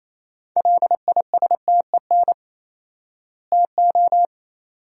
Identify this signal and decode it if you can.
Morse code, heard as a keyed tone, about 28 wpm, 700 Hz. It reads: LISTEN TO
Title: 4x Word-Spacing - Sentences from Top 400 Words - 28wpm